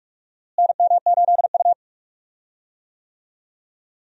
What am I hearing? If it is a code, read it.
Morse code, heard as a keyed tone, about 44 wpm, 700 Hz. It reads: NM8U
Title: USA Call Signs - 4 Characters - Repeated 2x - 45wpm